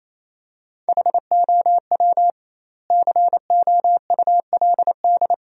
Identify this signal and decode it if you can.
Morse code, heard as a keyed tone, about 28 wpm, 700 Hz. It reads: HOW COULD
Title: 2x Word-Spacing - Sentences from Top 300 Words - 28wpm